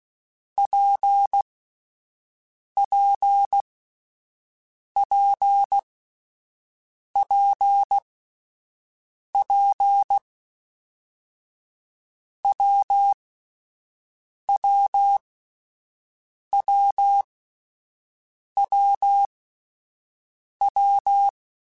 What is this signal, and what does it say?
Morse code, heard as a keyed tone, about 16 wpm, spaced as for 5 wpm, 800 Hz. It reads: PPPPP WWWWW
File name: Difficult Combos / Difficult Combos-mp3s / WPJ WPJ1 @5